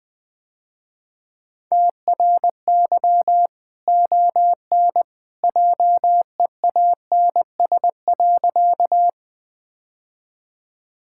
Morse code, heard as a keyed tone, about 20 wpm, 700 Hz. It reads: TRY ON JEANS.
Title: Easy Sentences with Hints - 20wpm